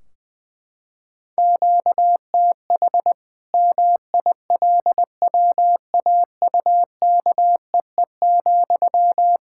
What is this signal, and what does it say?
Morse code, heard as a keyed tone, about 21 wpm, 700 Hz. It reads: QTH MILWAUKEE,
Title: K1USN SST QSO Practice 20wpm